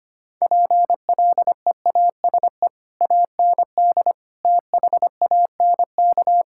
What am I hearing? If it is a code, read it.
Morse code, heard as a keyed tone, about 25 wpm, 700 Hz. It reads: PLEASE AND THANK
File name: practice / 25wpm / Difficult Sentences with Hints - 25wpm